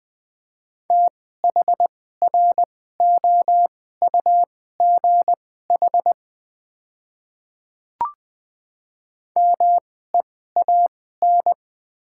Morse code, heard as a keyed tone, about 20 wpm, spaced as for 14 wpm, 700 Hz. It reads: THROUGH MEAN
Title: 2x Character Spacing - Top 200 Words - 20wpm